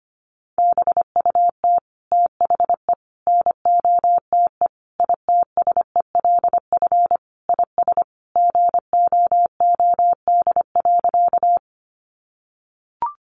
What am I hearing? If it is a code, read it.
Morse code, heard as a keyed tone, about 25 wpm, 700 Hz. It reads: BUT THE NOTE ITSELF IS GOOD.
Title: Sentences from Top 800 Words 25wpm